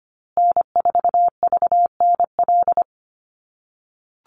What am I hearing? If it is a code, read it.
Morse code, heard as a keyed tone, about 25 wpm, 700 Hz. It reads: N4VNL
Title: USA Call Signs Repeated 3x 25wpm